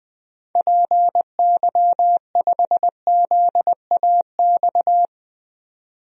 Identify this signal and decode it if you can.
Morse code, heard as a keyed tone, about 20 wpm, 700 Hz. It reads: PY5ZAX
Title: DXCC-100 Call Signs - 6 Characters - 20wpm